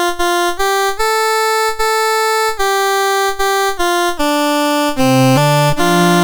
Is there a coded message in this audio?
No.